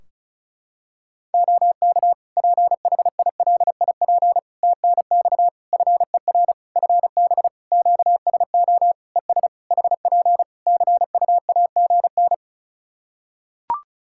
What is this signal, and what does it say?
Morse code, heard as a keyed tone, about 36 wpm, 700 Hz. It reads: OK PHILIP TNX FER FB QSO ES HP CUAGN
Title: Practice QSO - Exchange 4 - Ending - 35wpm